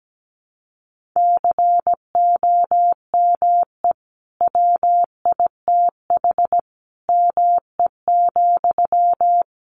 Morse code, heard as a keyed tone, about 17 wpm, 700 Hz. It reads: COME WITH ME,
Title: Sentences from Top 200 Words 17wpm